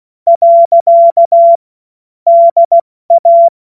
Morse code, heard as a keyed tone, about 16 wpm, 680 Hz. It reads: . DA
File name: Morsekurs / Witze / Witz 28. September 2021, 70 F80